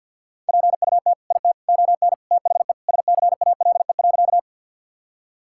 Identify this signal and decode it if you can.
Morse code, heard as a keyed tone, about 48 wpm, 700 Hz. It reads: PUT IT ON THE SCALE.
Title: Sentences from Top 1000 Words - Review Part 5 - 50wpm